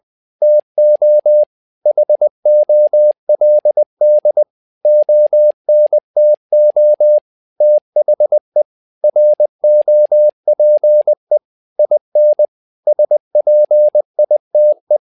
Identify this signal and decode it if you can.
Morse code, heard as a keyed tone, about 20 wpm, 600 Hz. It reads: TO HOLD ONTO THE ROPE IN SPITE
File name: tower trouble 20wpm